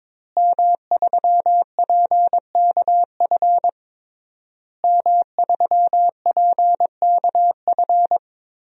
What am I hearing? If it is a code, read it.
Morse code, heard as a keyed tone, about 22 wpm, 700 Hz. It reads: M3PKF M3PKF
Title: DXCC-100 Call Signs - 5 Characters - Repeated 3x - 22wpm